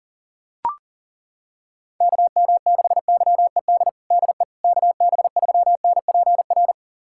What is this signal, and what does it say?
Morse code, heard as a keyed tone, about 37 wpm, 700 Hz. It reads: KM6YED DE KB3NPR
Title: USA Callsign DE Callsign - 6 Characters - 40wpm